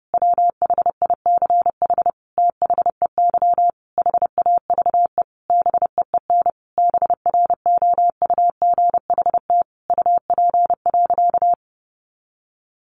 Morse code, heard as a keyed tone, about 28 wpm, 700 Hz. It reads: WHICH THEY HAVE BEEN BROUGHT UP.
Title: Sentences from Top 1500 Words 30wpm